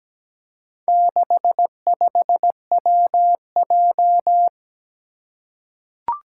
Morse code, heard as a keyed tone, about 17 wpm, 700 Hz. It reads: 65WJ
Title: Four Challenging Letter-Number Combinations 17wpm